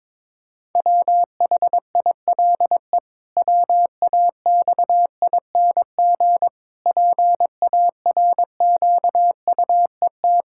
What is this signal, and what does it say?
Morse code, heard as a keyed tone, about 22 wpm, 700 Hz. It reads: WHILE WAXING PARQUET
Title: Pangram Challenge v2 - 22wpm